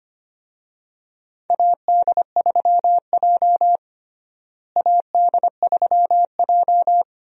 Morse code, heard as a keyed tone, about 25 wpm, 700 Hz. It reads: AD3J AD3J
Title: USA Call Signs - 4 Characters - Repeated 3x - 25wpm